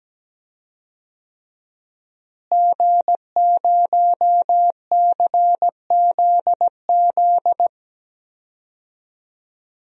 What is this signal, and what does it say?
Morse code, heard as a keyed tone, about 17 wpm, 700 Hz. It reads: G0CZZ
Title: DXCC-100 Call Signs - 5 Characters - Repeated 3x - 17wpm